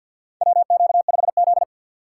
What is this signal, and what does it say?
Morse code, heard as a keyed tone, about 51 wpm, 700 Hz. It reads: WQ5Z
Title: USA Call Signs - 3x Character Spacing - 50wpm